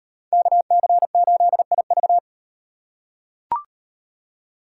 Morse code, heard as a keyed tone, about 36 wpm, 700 Hz. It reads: KC8IV